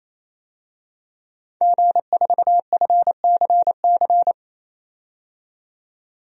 Morse code, heard as a keyed tone, about 28 wpm, 700 Hz. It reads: G4FCC
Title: DXCC-100 Call Signs - 5 Characters - 28wpm